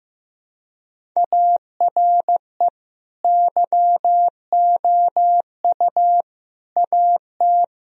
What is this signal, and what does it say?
Morse code, heard as a keyed tone, about 16 wpm, 700 Hz. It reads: ARE YOU AT
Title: Sentences from Top 1000 Words - Review Part 1 - 15wpm